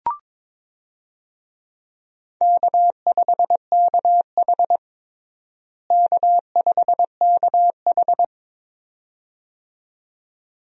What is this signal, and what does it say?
Morse code, heard as a keyed tone, about 22 wpm, 700 Hz. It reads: K5KH K5KH
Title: USA Call Signs - 4 Characters - Repeated 2x - 22wpm